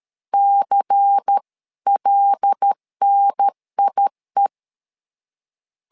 Morse code, heard as a keyed tone, about 13 wpm, 800 Hz. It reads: CLNIE
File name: LEZIONI / 5 LBVC / 5 LBVC 12wpm 1955